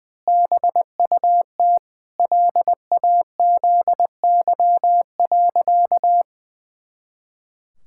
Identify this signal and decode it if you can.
Morse code, heard as a keyed tone, about 20 wpm, 700 Hz. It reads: BUT LAZY.
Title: Pangram Challenge v2 - 20wpm